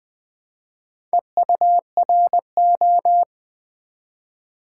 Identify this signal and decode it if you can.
Morse code, heard as a keyed tone, about 20 wpm, 700 Hz. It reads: EURO
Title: Build A Word - 1 Letter At A Time - 3 to 5-Letter Words - 20wpm